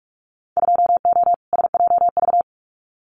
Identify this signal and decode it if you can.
Morse code, heard as a keyed tone, about 46 wpm, 700 Hz. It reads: 2O SJV
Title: Winter Field Day Exchanges Encore 45wpm